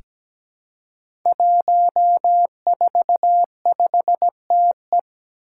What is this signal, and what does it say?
Morse code, heard as a keyed tone, about 17 wpm, 700 Hz. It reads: 145TE
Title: Five Challenging Letter-Number Combinations 17wpm